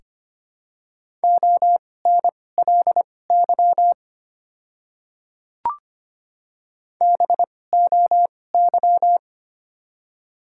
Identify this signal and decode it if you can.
Morse code, heard as a keyed tone, about 25 wpm, spaced as for 18 wpm, 700 Hz. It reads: ONLY BOY
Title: 2x Character Spacing - Top 200 Words - 25wpm